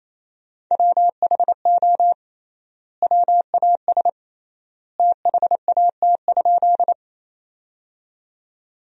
Morse code, heard as a keyed tone, about 28 wpm, 700 Hz. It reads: WHO WAS THAT?
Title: 3x Word-Spacing - Sentences from Top 100 Words - 28wpm